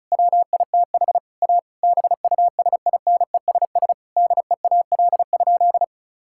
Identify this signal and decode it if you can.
Morse code, heard as a keyed tone, about 36 wpm, 700 Hz. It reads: WITH A BUSINESS DEAL?
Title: Sentences from Top 1000 Words - Review Part 6 - 35wpm